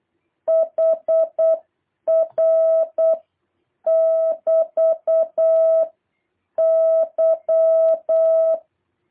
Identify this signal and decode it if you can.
Morse code, heard as a keyed tone, about 8 wpm, 660 Hz. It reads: HR <BT> Y
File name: jh8cbh / 2023cw